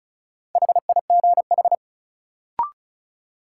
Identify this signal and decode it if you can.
Morse code, heard as a keyed tone, about 36 wpm, 700 Hz. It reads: HIGH
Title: Sets of 2 Words - New Word Drill - 35wpm